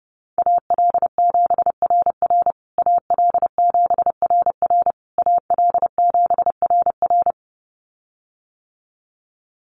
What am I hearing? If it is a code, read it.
Morse code, heard as a keyed tone, about 30 wpm, 700 Hz. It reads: AL7RR AL7RR AL7RR
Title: USA Call Signs Repeated 3x 30wpm